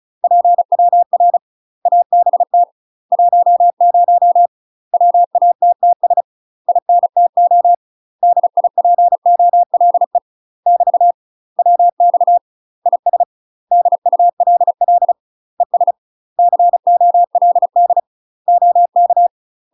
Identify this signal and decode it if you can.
Morse code, heard as a keyed tone, about 34 wpm, 700 Hz. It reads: PWR ABT 10 WATTS INTO DIPOLE <BT> WX IS DULL ES COLD OK
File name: UK MASTER QSOs TRAINING / 35WPM / UK MASTER QSOs 0042